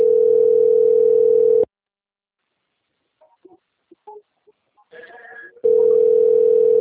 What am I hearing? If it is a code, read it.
Ringback tone (North America), 440 + 480 Hz.